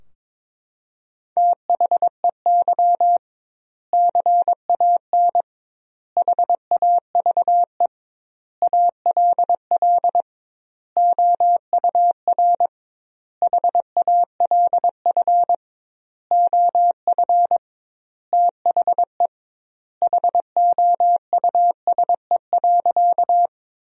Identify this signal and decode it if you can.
Morse code, heard as a keyed tone, about 22 wpm, 700 Hz. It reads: THEY CAN HAVE ALL OUR HALF OF THE HOUSE.
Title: 2x Word-Spacing - Sentences from Top 400 Words - 22wpm